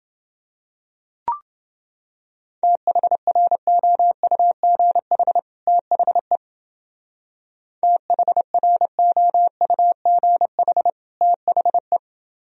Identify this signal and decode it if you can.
Morse code, heard as a keyed tone, about 28 wpm, 700 Hz. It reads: THROUGH THE THROUGH THE